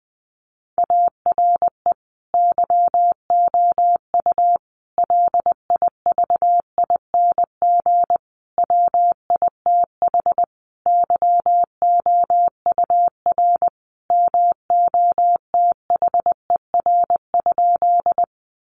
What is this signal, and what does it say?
Morse code, heard as a keyed tone, about 19 wpm, 700 Hz. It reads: ARE YOU LIVING WITH YOUR MOTHER?
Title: Sentences from Top 1100 Words 20wpm